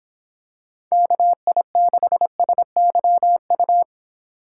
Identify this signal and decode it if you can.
Morse code, heard as a keyed tone, about 26 wpm, 700 Hz. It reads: KI6SYU